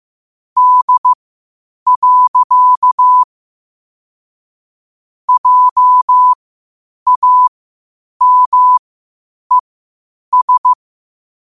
Morse code, heard as a keyed tone, about 15 wpm, spaced as for 8 wpm, 1000 Hz. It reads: D. JAMES
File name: qotdfiles / 8wpm